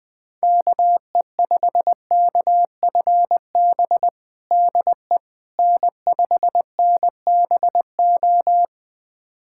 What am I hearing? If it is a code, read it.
Morse code, heard as a keyed tone, about 20 wpm, 700 Hz. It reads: KE5KFB DE N5NBO
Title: USA Callsign DE Callsign 20wpm